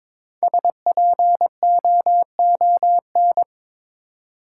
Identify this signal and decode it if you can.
Morse code, heard as a keyed tone, about 22 wpm, 700 Hz. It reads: SPOON
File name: 5 Letter Words - First 3 Letters Given - 22wpm